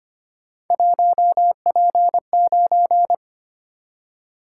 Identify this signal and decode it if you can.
Morse code, heard as a keyed tone, about 25 wpm, 700 Hz. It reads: 1P9
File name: Three Letter-Number Combinations - 25wpm